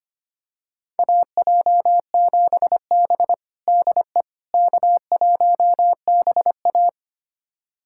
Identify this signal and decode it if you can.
Morse code, heard as a keyed tone, about 25 wpm, 700 Hz. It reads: AJ7B DE K1BA
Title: USA Callsign DE Callsign - 4 Characters - 25wpm